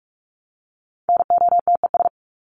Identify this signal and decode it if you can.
Morse code, heard as a keyed tone, about 44 wpm, 700 Hz. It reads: NOTES
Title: ES:S Suffix - Top 100 Words with Base Words 45wpm